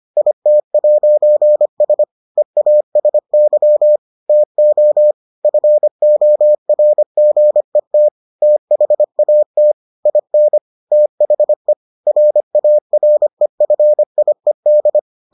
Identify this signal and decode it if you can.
Morse code, heard as a keyed tone, about 25 wpm, 600 Hz. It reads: IT'S EASY TO FORGET THAT IN THE RAREFIED